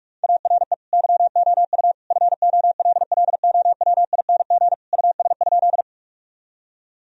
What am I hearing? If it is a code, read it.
Morse code, heard as a keyed tone, about 46 wpm, 700 Hz. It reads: ARE YOU FOLLOWING US?
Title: Sentences from Top 1000 Words - Review Part 4 - 45wpm